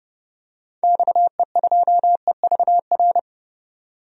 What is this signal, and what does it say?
Morse code, heard as a keyed tone, about 31 wpm, 700 Hz. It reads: XE2EVR
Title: DXCC-100 Call Signs - 6 Characters - 30wpm